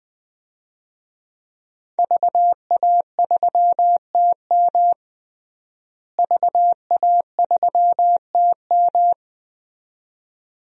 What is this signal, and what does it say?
Morse code, heard as a keyed tone, about 20 wpm, 700 Hz. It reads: VA3TM VA3TM
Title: DXCC-100 Call Signs - 5 Characters - Repeated 2x - 20wpm